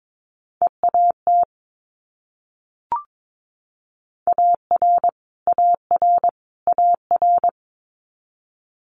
Morse code, heard as a keyed tone, about 22 wpm, 700 Hz. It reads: EAT AR AR AR
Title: Multi-letter Phonemes Repeated 3x 22wpm